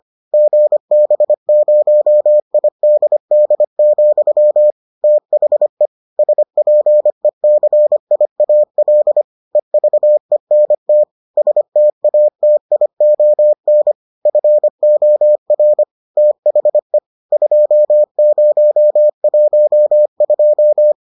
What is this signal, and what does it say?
Morse code, heard as a keyed tone, about 25 wpm, 600 Hz. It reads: GB0IDD, THE SPECIAL EVENT STATION FOR THE 2012